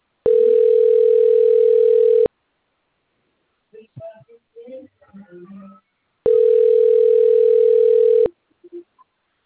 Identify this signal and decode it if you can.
Ringback tone (North America), 440 + 480 Hz.